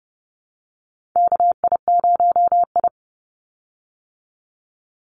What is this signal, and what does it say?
Morse code, heard as a keyed tone, about 30 wpm, 700 Hz. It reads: KI0I
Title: Call Signs Repeated 2x - 30wpm